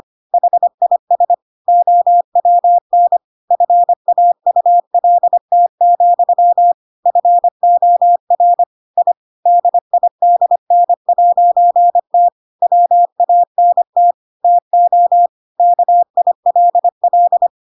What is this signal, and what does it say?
Morse code, heard as a keyed tone, about 25 wpm, 700 Hz. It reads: HIS OWN FAULT, FOR I DIDN'T WANT TO KILL
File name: Pinocchio / Pinocchio ch7 25 25 ik1hge